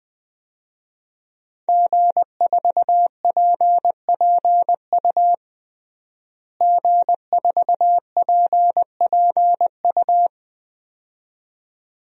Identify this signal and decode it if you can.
Morse code, heard as a keyed tone, about 20 wpm, 700 Hz. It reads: G4PPU G4PPU
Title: DXCC-100 Call Signs - 5 Characters - Repeated 2x - 20wpm